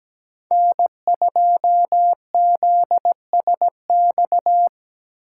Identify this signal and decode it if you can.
Morse code, heard as a keyed tone, about 20 wpm, 700 Hz. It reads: N2ZSX
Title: Speed Racing - USA 1x3 Call Signs 17wpm